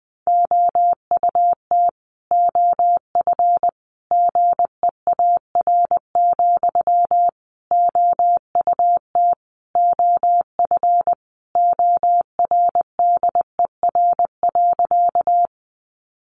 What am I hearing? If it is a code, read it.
Morse code, heard as a keyed tone, about 20 wpm, 700 Hz. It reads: OUT OF GEAR, OUT OF ORDER.
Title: Sentences from Top 1400 Words 20wpm